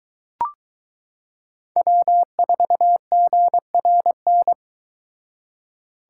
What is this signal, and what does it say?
Morse code, heard as a keyed tone, about 22 wpm, 700 Hz. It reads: W4GRN